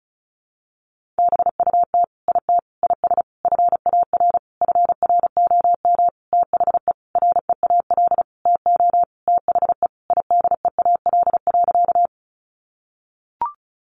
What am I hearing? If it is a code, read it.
Morse code, heard as a keyed tone, about 34 wpm, 700 Hz. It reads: BUT IT IS FAR FROM THE REAL TO THE IDEAL.
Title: Sentences from Top 1400 Words 35wpm